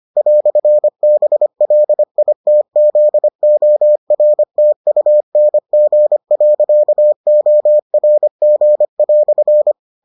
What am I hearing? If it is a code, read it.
Morse code, heard as a keyed tone, about 25 wpm, 600 Hz. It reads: "BLITZORTUNG.ORG"